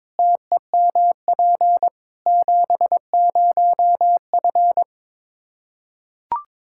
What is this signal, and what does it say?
Morse code, heard as a keyed tone, about 22 wpm, 700 Hz. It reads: TEMP 70F
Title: Practice QSO - Exchange 2 - Info - 22wpm